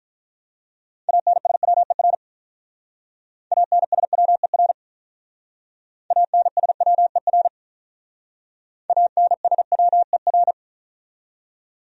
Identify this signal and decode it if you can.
Morse code, heard as a keyed tone, about 43 wpm, 700 Hz. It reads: ANSWER ANSWER ANSWER ANSWER
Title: Speed-Racing Top 300 Words 35wpm